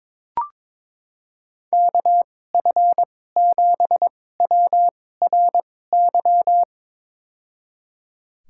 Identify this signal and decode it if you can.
Morse code, heard as a keyed tone, about 22 wpm, spaced as for 16 wpm, 700 Hz. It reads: KF7WRY
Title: USA Call Signs - 6 Characters - 2x Character Spacing - 22wpm